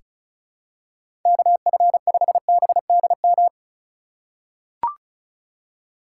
Morse code, heard as a keyed tone, about 34 wpm, 700 Hz. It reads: KF5BDM
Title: USA Call Signs - 6 Characters - 2x Character Spacing - 35wpm